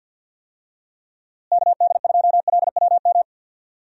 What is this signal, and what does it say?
Morse code, heard as a keyed tone, about 50 wpm, 700 Hz. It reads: KD2LWM
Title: USA Call Signs - 6 Characters - Repeated 3x - 50wpm